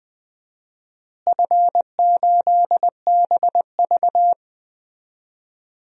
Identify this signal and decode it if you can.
Morse code, heard as a keyed tone, about 21 wpm, 700 Hz. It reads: F8BV
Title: DXCC-100 Call Signs - 4 Characters - Repeated 3x - 20wpm